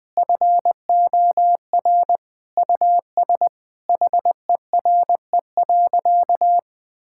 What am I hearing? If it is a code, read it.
Morse code, heard as a keyed tone, about 20 wpm, 700 Hz. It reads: FOR US HERE.
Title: Sentences from Top 1000 Words - Review Part 5 - 20wpm